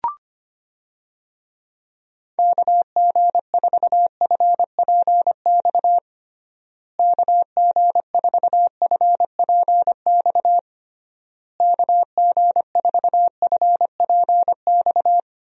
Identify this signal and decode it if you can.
Morse code, heard as a keyed tone, about 25 wpm, 700 Hz. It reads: KG4FPX KG4FPX KG4FPX
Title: USA Call Signs - 6 Characters - Repeated 3x - 25wpm